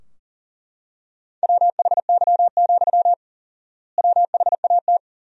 Morse code, heard as a keyed tone, about 40 wpm, 700 Hz. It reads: WHY, WHAT
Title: 4x Word-Spacing - Sentences from Top 200 Words - 40wpm